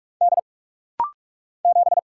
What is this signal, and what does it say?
Morse code, heard as a keyed tone, about 44 wpm, 700 Hz. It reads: D 7
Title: Single Letter-Number Alt - Rapid-Fire 45wpm